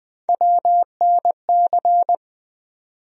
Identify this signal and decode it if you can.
Morse code, heard as a keyed tone, about 20 wpm, 700 Hz. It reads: WNC
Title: USA Call Sign Postfixes - 3 Letters 20wpm